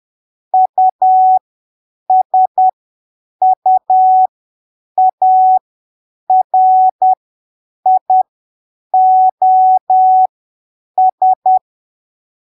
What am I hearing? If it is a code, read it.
Morse code, heard as a keyed tone, about 10 wpm, spaced as for 7 wpm, 750 Hz. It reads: USUARIOS
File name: arquivos / texto / texto 06 10ppm